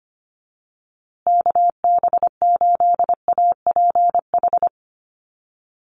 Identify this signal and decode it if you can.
Morse code, heard as a keyed tone, about 25 wpm, 700 Hz. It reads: KB8APH
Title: USA Call Signs Repeated 3x 25wpm